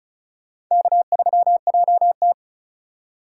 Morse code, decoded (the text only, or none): K3JT